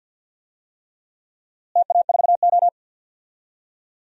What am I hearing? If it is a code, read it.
Morse code, heard as a keyed tone, about 52 wpm, 700 Hz. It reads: TA4O
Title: DXCC-100 Call Signs - 4 Characters - Repeated 3x - 50wpm